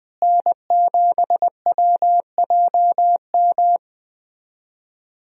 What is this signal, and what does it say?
Morse code, heard as a keyed tone, about 20 wpm, 700 Hz. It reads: N7WJM
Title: USA Call Signs - 5 Characters - Repeated 2x - 20wpm